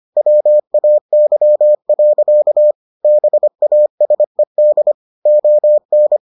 Morse code, heard as a keyed tone, about 25 wpm, 600 Hz. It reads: WAY. BASED ON